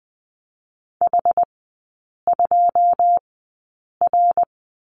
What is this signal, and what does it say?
Morse code, heard as a keyed tone, about 20 wpm, spaced as for 8 wpm, 700 Hz. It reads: H2R
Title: 093 Lesson 16e - Letter c - Three Characters 20wpm